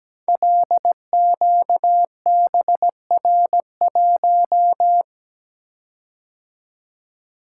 Morse code, heard as a keyed tone, about 17 wpm, 700 Hz. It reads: LQBR1
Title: Code Groups - 5 Letter:Numbers - 17wpm